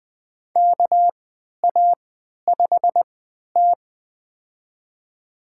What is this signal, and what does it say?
Morse code, heard as a keyed tone, about 20 wpm, spaced as for 11 wpm, 700 Hz. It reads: KA5T
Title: USA Call Signs - 4 Characters - 3x Character Spacing - 20wpm